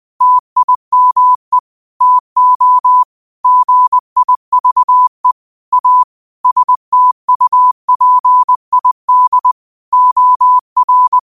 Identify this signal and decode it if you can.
Morse code, heard as a keyed tone, about 20 wpm, 1000 Hz. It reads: TIME TO GIVE A STUPID OR
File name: 20wpm